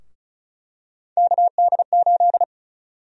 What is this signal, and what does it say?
Morse code, heard as a keyed tone, about 36 wpm, 700 Hz. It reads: KD8
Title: USA Call Sign Prefixes - Repeated 2x - 35wpm